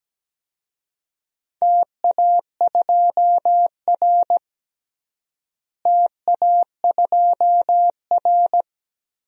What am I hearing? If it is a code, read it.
Morse code, heard as a keyed tone, about 17 wpm, 700 Hz. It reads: TA2R TA2R
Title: DXCC-100 Call Signs - 4 Characters - Repeated 2x - 17wpm